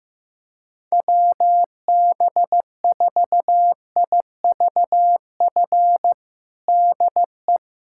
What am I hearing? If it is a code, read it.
Morse code, heard as a keyed tone, about 15 wpm, 700 Hz. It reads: WB4IVF DE
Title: USA Callsign DE Callsign 15wpm